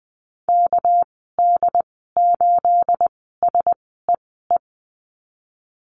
Morse code, heard as a keyed tone, about 20 wpm, spaced as for 15 wpm, 700 Hz. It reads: KD8SEE
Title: 219 Lesson 35g - Number 0 - Call Signs 20wpm